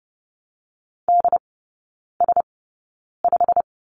Morse code, heard as a keyed tone, about 30 wpm, spaced as for 9 wpm, 700 Hz. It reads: DS5
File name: course / 30:10wpm / 087 Lesson 15e - Number 5 - Three Characters 30wpm